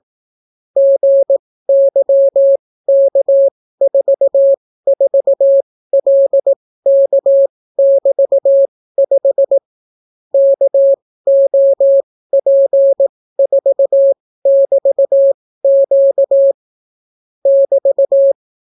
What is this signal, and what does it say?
Morse code, heard as a keyed tone, about 18 wpm, spaced as for 14 wpm, 550 Hz. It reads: GYK44LK<BT>5 KOP4<BT>Q <BT>